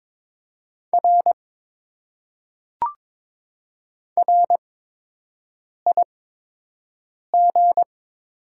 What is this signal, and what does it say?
Morse code, heard as a keyed tone, about 22 wpm, spaced as for 6 wpm, 700 Hz. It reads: R RIG
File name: Top 90 QSO Elements - 8x Character Spacing - 22wpm